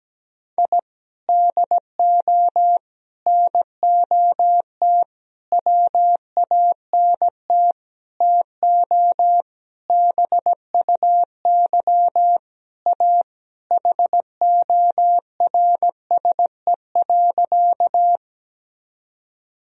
Morse code, heard as a keyed tone, about 17 wpm, 700 Hz. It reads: I DO NOT WANT TO BUY A HORSE.
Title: Sentences from Top 1000 Words - Review Part 4 - 17wpm